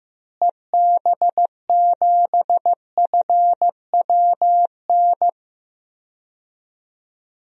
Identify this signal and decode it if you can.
Morse code, heard as a keyed tone, about 15 wpm, 700 Hz. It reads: EB7FWN
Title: DXCC-100 Call Signs - 6 Characters - Repeated 2x - 15wpm